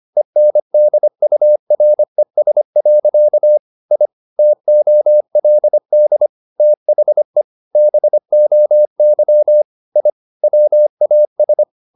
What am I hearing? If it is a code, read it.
Morse code, heard as a keyed tone, about 25 wpm, 600 Hz. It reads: ENDURES. I TOLD THE BOY I WAS
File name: the old man and the sea part 1 25wpm